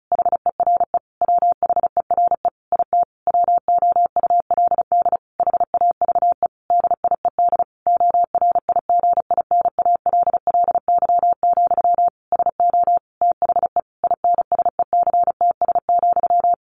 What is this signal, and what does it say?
Morse code, heard as a keyed tone, about 34 wpm, 700 Hz. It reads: HERE WHERE IT WOULD HAVE DIED ORIGINALLY, SO THE INSECTS,
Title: A Sound Of Thunder 35wpm